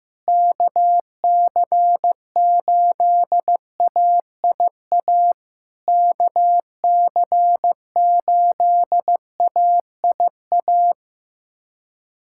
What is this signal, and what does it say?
Morse code, heard as a keyed tone, about 15 wpm, 700 Hz. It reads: KC8AIA KC8AIA
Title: SOTA QSO Practice 15wpm